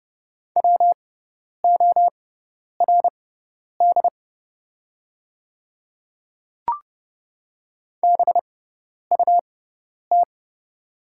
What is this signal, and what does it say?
Morse code, heard as a keyed tone, about 31 wpm, spaced as for 10 wpm, 700 Hz. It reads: WORD BUT
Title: 6x Character Spacing - Top 100 Words - 30wpm